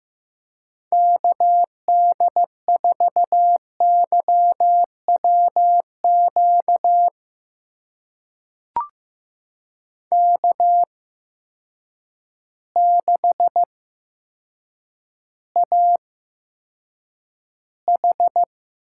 Morse code, heard as a keyed tone, about 15 wpm, spaced as for 8 wpm, 700 Hz. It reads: KD4YWQ K6AH
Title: USA Call Signs - 8x Character Spacing - 15wpm